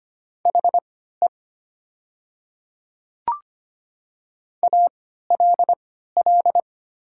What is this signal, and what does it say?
Morse code, heard as a keyed tone, about 25 wpm, spaced as for 14 wpm, 700 Hz. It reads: HE ALL
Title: 3x Character Spacing - Top 100 Words - 25wpm